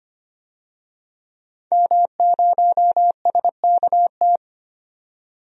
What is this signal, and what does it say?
Morse code, heard as a keyed tone, about 25 wpm, 700 Hz. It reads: M0SKT
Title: DXCC-100 Call Signs - 5 Characters - 25wpm